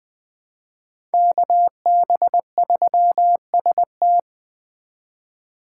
Morse code, heard as a keyed tone, about 24 wpm, 700 Hz. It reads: KB3ST
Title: Speed Racing - USA 2x2 Call Signs 20wpm